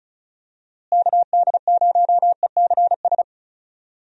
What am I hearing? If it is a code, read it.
Morse code, heard as a keyed tone, about 34 wpm, 700 Hz. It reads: KD0ECS
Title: USA Call Signs - 6 Characters - Repeated 3x - 35wpm